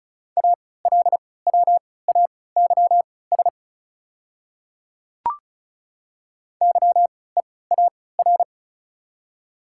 Morse code, heard as a keyed tone, about 36 wpm, spaced as for 20 wpm, 700 Hz. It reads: ALWAYS YEAR
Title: 3x Character Spacing - Top 300 Words - 35wpm